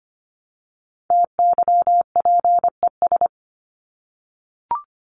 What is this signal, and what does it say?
Morse code, heard as a keyed tone, about 25 wpm, 700 Hz. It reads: TYPES
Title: ES:S Suffix - Top 100 Words with Base Words 25wpm